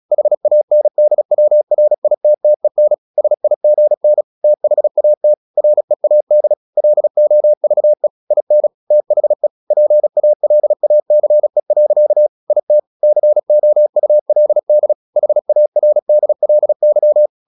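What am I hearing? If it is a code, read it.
Morse code, heard as a keyed tone, about 36 wpm, 600 Hz. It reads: HANDWRITTEN SIGN THAT READ LOVE IN THE PALACE. IT COULD HARDLY
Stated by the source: Morse practice at 36 wpm, text of Ken Follett's novel Never, from CW med Gnister.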